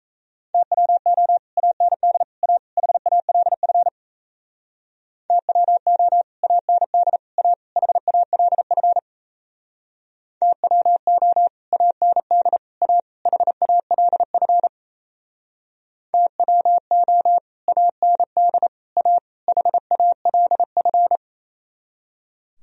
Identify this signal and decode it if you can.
Morse code, heard as a keyed tone, about 36 wpm, 700 Hz. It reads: TWO AND A HALF TWO AND A HALF TWO AND A HALF TWO AND A HALF